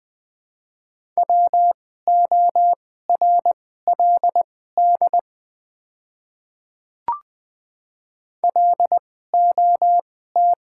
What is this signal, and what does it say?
Morse code, heard as a keyed tone, about 20 wpm, spaced as for 14 wpm, 700 Hz. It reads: WORLD LOT
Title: 2x Character Spacing - Top 500 Words - 20wpm